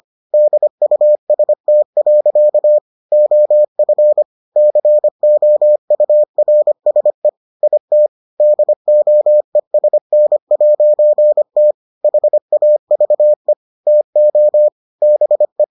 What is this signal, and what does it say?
Morse code, heard as a keyed tone, about 25 wpm, 600 Hz. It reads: DUST. OF COURSE IT DOESN'T HAVE TO BE